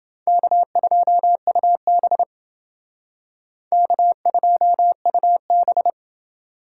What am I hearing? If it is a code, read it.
Morse code, heard as a keyed tone, about 29 wpm, 700 Hz. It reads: K2UB K2UB